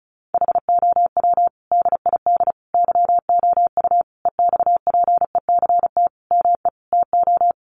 Morse code, heard as a keyed tone, about 34 wpm, 700 Hz. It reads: HOW DID YOU EXPECT ME TO
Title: Sentences from Top 1100 Words 35wpm